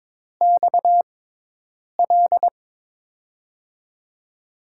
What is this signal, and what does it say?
Morse code, heard as a keyed tone, about 22 wpm, spaced as for 8 wpm, 700 Hz. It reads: XL
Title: USA Call Signs - 6 Characters - 6x Character Spacing - 22wpm